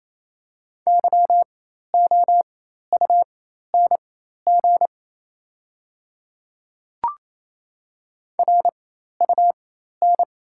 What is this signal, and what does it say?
Morse code, heard as a keyed tone, about 27 wpm, spaced as for 13 wpm, 700 Hz. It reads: YOUNG RUN
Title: 4x Character Spacing - Top 500 Words - 28wpm